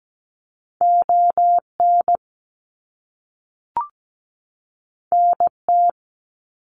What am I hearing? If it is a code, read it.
Morse code, heard as a keyed tone, about 17 wpm, 700 Hz. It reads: ON NT
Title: Canadian Provinces and Territories 17wpm